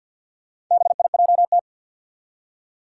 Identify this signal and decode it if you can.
Morse code, heard as a keyed tone, about 47 wpm, 700 Hz. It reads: BIJT
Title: Four Challenging Letter-Number Combinations 50wpm